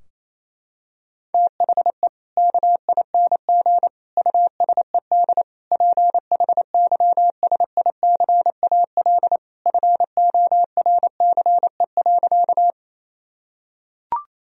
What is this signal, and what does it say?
Morse code, heard as a keyed tone, about 28 wpm, 700 Hz. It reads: THE KING USED PHYSICAL FORCE.
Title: Sentences from Top 1000 Words - Review Part 4 - 28wpm